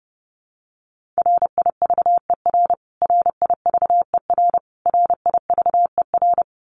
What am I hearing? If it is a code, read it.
Morse code, heard as a keyed tone, about 30 wpm, 700 Hz. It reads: RIVER RIVER RIVER
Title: Top 300 Words Repeated 30wpm